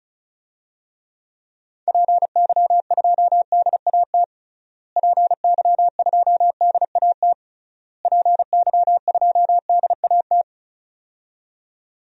Morse code, heard as a keyed tone, about 34 wpm, 700 Hz. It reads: PY2DAT PY2DAT PY2DAT
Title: DXCC-100 Call Signs - 6 Characters - Repeated 3x - 35wpm